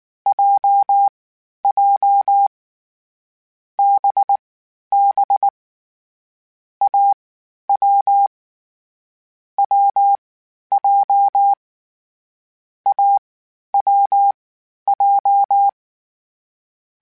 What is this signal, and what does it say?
Morse code, heard as a keyed tone, about 19 wpm, spaced as for 11 wpm, 800 Hz. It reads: JJ BB AW WJ AWJ